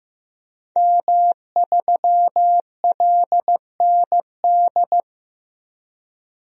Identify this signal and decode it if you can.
Morse code, heard as a keyed tone, about 15 wpm, 700 Hz. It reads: M3LND
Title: DXCC-100 Call Signs - 5 Characters - Repeated 3x - 15wpm